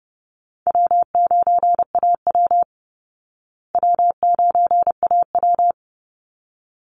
Morse code, heard as a keyed tone, about 30 wpm, 700 Hz. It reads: W9AW W9AW
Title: Call Signs Repeated 3x - 30wpm